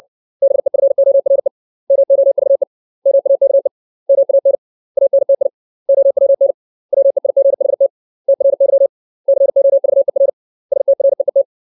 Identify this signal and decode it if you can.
Morse code, heard as a keyed tone, about 60 wpm, 550 Hz. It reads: BEFORE MOVE MAKE MAN ANTI OWN WEIGHT TRY COVER STREET
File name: 632 words / 632 Words 60WPM